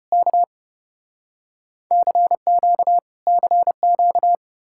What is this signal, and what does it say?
Morse code, heard as a keyed tone, about 35 wpm, 700 Hz. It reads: K CQ CQ